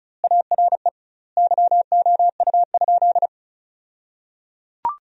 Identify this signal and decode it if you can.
Morse code, heard as a keyed tone, about 35 wpm, 700 Hz. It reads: ARE YOU?
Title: 2x Word-Spacing - Sentences from Top 100 Words - 35wpm